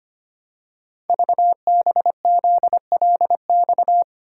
Morse code, heard as a keyed tone, about 25 wpm, 700 Hz. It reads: VBZLX
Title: Code Groups - 5 Letters - 25wpm